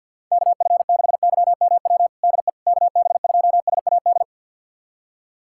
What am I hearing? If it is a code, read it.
Morse code, heard as a keyed tone, about 52 wpm, 700 Hz. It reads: KF6YMW DE KB2SAD
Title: USA Callsign DE Callsign 50wpm